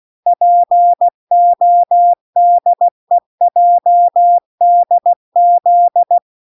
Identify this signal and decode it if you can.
Morse code, heard as a keyed tone, about 16 wpm, 700 Hz. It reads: PODEJDZ